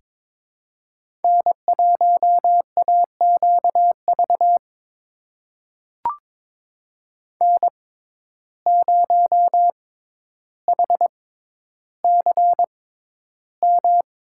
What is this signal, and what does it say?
Morse code, heard as a keyed tone, about 22 wpm, spaced as for 11 wpm, 700 Hz. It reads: N1AQV N0HCM
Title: USA Call Signs - 5 Characters - 6x Character Spacing - 22wpm